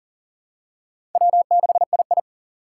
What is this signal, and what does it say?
Morse code, heard as a keyed tone, about 40 wpm, 700 Hz. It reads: W6II
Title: USA Call Signs - 4 Characters - 3x Character Spacing - 40wpm